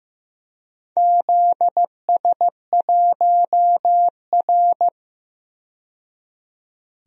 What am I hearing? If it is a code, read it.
Morse code, heard as a keyed tone, about 15 wpm, 700 Hz. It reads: ZS1R
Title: DXCC-100 Call Signs - 4 Characters - 15wpm